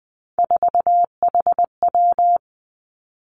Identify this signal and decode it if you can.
Morse code, heard as a keyed tone, about 20 wpm, 700 Hz. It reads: 4HW
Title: Three Challenging Letter-Number Combinations 20wpm